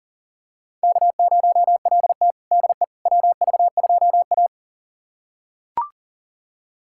Morse code, heard as a keyed tone, about 40 wpm, 700 Hz. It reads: K0LT DE WV2A
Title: USA Callsign DE Callsign - 4 Characters - 40wpm